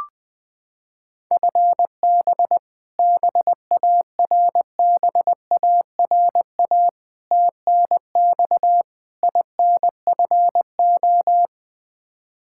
Morse code, heard as a keyed tone, about 20 wpm, 700 Hz. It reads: FB BARBARA TNX INFO
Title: Practice QSO - Exchange 3 - More Info - 20wpm